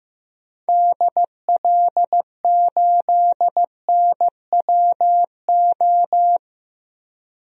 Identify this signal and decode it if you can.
Morse code, heard as a keyed tone, about 15 wpm, 700 Hz. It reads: DL8NWO
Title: DXCC-100 Call Signs - 6 Characters - 15wpm